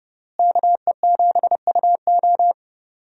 Morse code, heard as a keyed tone, about 30 wpm, 700 Hz. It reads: KE7UO
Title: USA Call Signs - 5 Characters - Repeated 3x - 30wpm